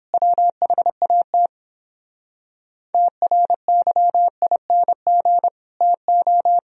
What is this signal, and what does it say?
Morse code, heard as a keyed tone, about 31 wpm, 700 Hz. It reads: WHAT TRYING TO